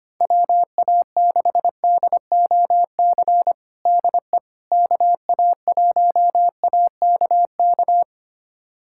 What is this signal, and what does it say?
Morse code, heard as a keyed tone, about 25 wpm, 700 Hz. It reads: WA6DOC DE KA1AKK
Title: USA Callsign DE Callsign - 6 Characters - 25wpm